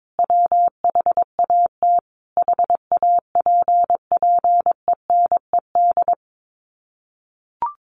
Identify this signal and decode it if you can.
Morse code, heard as a keyed tone, about 22 wpm, 700 Hz. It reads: WHAT HAPPENED
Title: Sets of 5 Words Encore2 22wpm